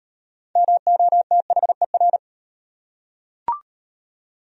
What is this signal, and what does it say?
Morse code, heard as a keyed tone, about 32 wpm, 700 Hz. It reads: MOTHER